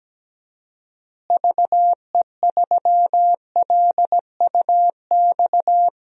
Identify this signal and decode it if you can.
Morse code, heard as a keyed tone, about 17 wpm, 700 Hz. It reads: VE3LUX
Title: DXCC-100 Call Signs - 6 Characters - Repeated 3x - 17wpm